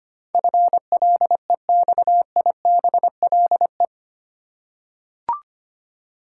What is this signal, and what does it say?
Morse code, heard as a keyed tone, about 25 wpm, 690 Hz. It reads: FLEXIBLE
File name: 8 Letter Words - First 3 Letters Given - 25wpm